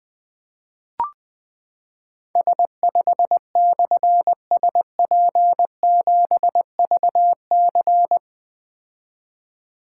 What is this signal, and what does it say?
Morse code, heard as a keyed tone, about 21 wpm, 700 Hz. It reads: S5/SP7VC
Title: Difficult Call Signs 20wpm